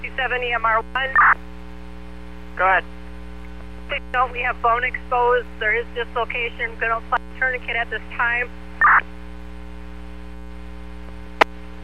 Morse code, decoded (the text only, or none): none